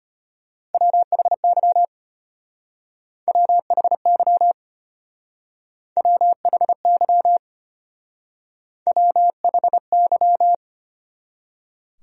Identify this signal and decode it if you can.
Morse code, heard as a keyed tone, about 30 wpm, 700 Hz. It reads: WHY WHY WHY WHY
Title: Speed-Racing Top 300 Words 25wpm